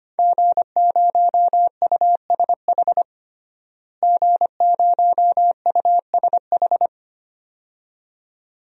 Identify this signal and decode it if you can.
Morse code, heard as a keyed tone, about 25 wpm, 700 Hz. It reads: G0USH G0USH
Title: DXCC-100 Call Signs - 5 Characters - Repeated 2x - 25wpm